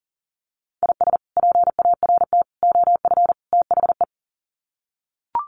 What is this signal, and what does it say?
Morse code, heard as a keyed tone, about 40 wpm, 700 Hz. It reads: IS PART OF THE
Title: Sets of 4 Words 40wpm